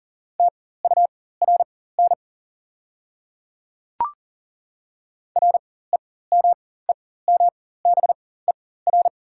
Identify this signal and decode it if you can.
Morse code, heard as a keyed tone, about 40 wpm, spaced as for 19 wpm, 690 Hz. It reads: TURN REMEMBER
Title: 4x Character Spacing - Top 500 Words - 40wpm